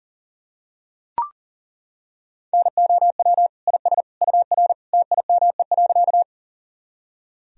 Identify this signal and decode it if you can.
Morse code, heard as a keyed tone, about 40 wpm, 700 Hz. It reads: NOW IS UR TIME.
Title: Sentences from Top 100 Words and Abbreviations 40wpm